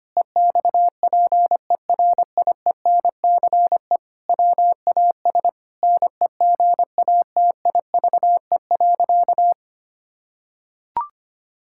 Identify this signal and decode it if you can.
Morse code, heard as a keyed tone, about 25 wpm, 700 Hz. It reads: EXPERIENCE WAS NEGATIVE.
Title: Sentences from Top 1000 Words - Review Part 6 - 25wpm